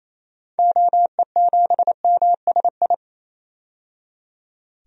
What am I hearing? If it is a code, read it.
Morse code, heard as a keyed tone, about 28 wpm, 700 Hz. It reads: OE7MSI
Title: DXCC-100 Call Signs - 6 Characters - 28wpm